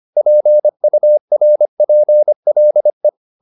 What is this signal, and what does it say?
Morse code, heard as a keyed tone, about 25 wpm, 600 Hz. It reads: PURPLE
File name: the old man and the sea part 1 25wpm